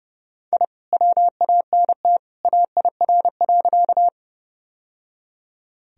Morse code, heard as a keyed tone, about 30 wpm, 700 Hz. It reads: I WANT AIR.
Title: Sentences from Top 1000 Words - Review Part 1 - 30wpm